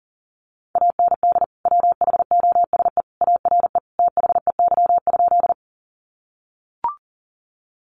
Morse code, heard as a keyed tone, about 40 wpm, 700 Hz. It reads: AND WHOSE ARE THEY?
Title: Sentences from Top 1200 Words 40wpm